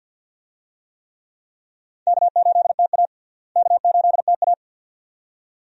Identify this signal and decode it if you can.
Morse code, heard as a keyed tone, about 52 wpm, 700 Hz. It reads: K8TA K8TA
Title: USA Call Signs - 4 Characters - Repeated 2x - 50wpm